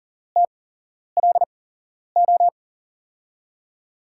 Morse code, heard as a keyed tone, about 40 wpm, spaced as for 11 wpm, 700 Hz. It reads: TLO
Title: USA Call Signs - 5 Characters - 8x Character Spacing - 40wpm